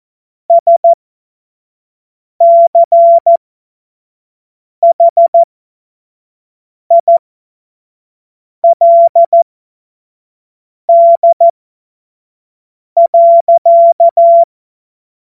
Morse code, heard as a keyed tone, about 14 wpm, spaced as for 5 wpm, 680 Hz. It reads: SCHILD.
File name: Witz-19-August-2025-25-F70